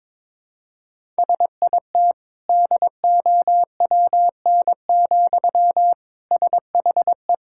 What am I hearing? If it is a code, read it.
Morse code, heard as a keyed tone, about 22 wpm, 700 Hz. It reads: SIT DOWN, SHE
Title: Sentences from Top 1000 Words - Review Part 6 - 22wpm